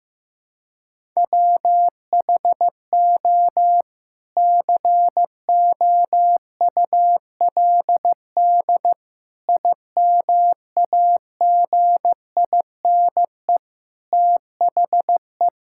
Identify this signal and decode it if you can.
Morse code, heard as a keyed tone, about 15 wpm, 700 Hz. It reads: WHO COULD IMAGINE THE
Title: Sentences from Top 1600 Words 15wpm